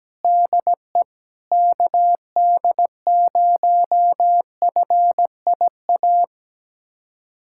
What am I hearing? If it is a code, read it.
Morse code, heard as a keyed tone, about 17 wpm, 700 Hz. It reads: DE KD0FIA
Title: USA Callsign DE Callsign 17wpm